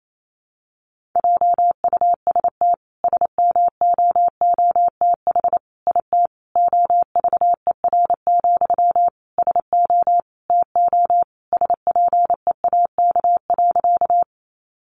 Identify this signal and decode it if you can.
Morse code, heard as a keyed tone, about 28 wpm, 700 Hz. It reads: JUST SMOOTH IT OVER, SO TO SPEAK.
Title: Sentences from Top 1400 Words 28wpm